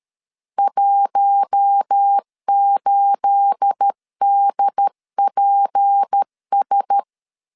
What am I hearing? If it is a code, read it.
Morse code, heard as a keyed tone, about 13 wpm, 800 Hz. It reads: 18DPS
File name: LEZIONI / 9 28 12wpm 2351